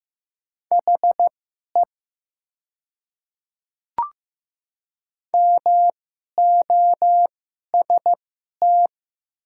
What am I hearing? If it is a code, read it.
Morse code, heard as a keyed tone, about 15 wpm, spaced as for 11 wpm, 700 Hz. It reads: HE MOST
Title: 2x Character Spacing - Top 100 Words - 15wpm